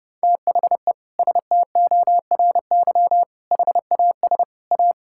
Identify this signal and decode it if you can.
Morse code, heard as a keyed tone, about 32 wpm, 700 Hz. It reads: THE STORY HAS A
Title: Sentences from Top 1000 Words - Review Part 6 - 30wpm